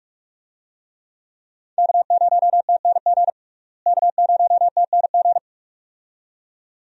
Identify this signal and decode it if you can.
Morse code, heard as a keyed tone, about 43 wpm, 700 Hz. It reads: K0TNG K0TNG
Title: USA Call Signs - 5 Characters - Repeated 2x - 45wpm